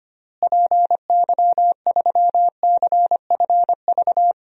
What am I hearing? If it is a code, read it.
Morse code, heard as a keyed tone, about 25 wpm, 700 Hz. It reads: PY3CFV
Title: DXCC-100 Call Signs - 6 Characters - Repeated 2x - 25wpm